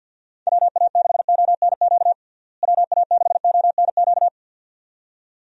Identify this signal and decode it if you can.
Morse code, heard as a keyed tone, about 53 wpm, 700 Hz. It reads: WA6ONQ WA6ONQ
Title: USA Call Signs - 6 Characters - Repeated 2x - 50wpm